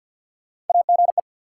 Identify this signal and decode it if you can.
Morse code, heard as a keyed tone, about 53 wpm, 700 Hz. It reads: AGE
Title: Sets of 3 Words - New Word Drill - 50wpm